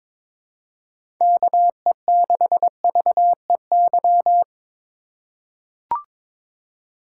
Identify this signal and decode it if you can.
Morse code, heard as a keyed tone, about 21 wpm, 700 Hz. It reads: KE6VEY
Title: USA Call Signs - 6 Characters - 3x Character Spacing - 22wpm